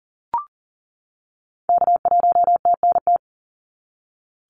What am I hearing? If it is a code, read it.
Morse code, heard as a keyed tone, about 40 wpm, 700 Hz. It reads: K1TNT
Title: NA SOTA Activator Callsigns 40wpm